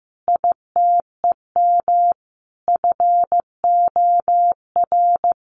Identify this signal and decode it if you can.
Morse code, heard as a keyed tone, about 14 wpm, 700 Hz. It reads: ITEM FOR
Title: Sentences from Top 1400 Words 15wpm